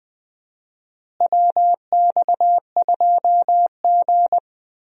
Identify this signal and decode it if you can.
Morse code, heard as a keyed tone, about 20 wpm, 700 Hz. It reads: WX2G
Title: USA Call Signs - 4 Characters - Repeated 2x - 20wpm